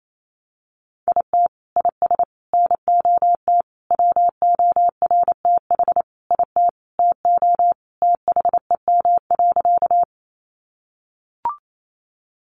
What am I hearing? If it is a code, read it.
Morse code, heard as a keyed tone, about 28 wpm, 700 Hz. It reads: IT IS NOT WORTH IT TO THEM.
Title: Sentences from Top 800 Words 28wpm